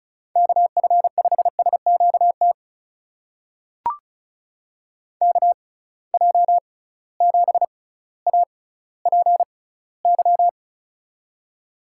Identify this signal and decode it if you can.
Morse code, heard as a keyed tone, about 34 wpm, spaced as for 18 wpm, 700 Hz. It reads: KF5SQT KJ7APY
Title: USA Call Signs - 6 Characters - 6x Character Spacing - 35wpm